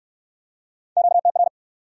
Morse code, heard as a keyed tone, about 59 wpm, 700 Hz. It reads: QTR